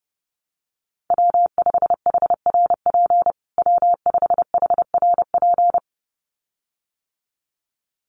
Morse code, heard as a keyed tone, about 30 wpm, 700 Hz. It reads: W5HRP W5HRP
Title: USA Call Signs Repeated 2x 30wpm